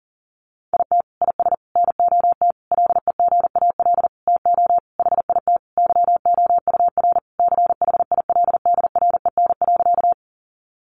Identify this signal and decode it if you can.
Morse code, heard as a keyed tone, about 40 wpm, 700 Hz. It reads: IT IS NOT LEGAL TO HIT YOUR CHILDREN.
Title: Sentences from Top 1200 Words 40wpm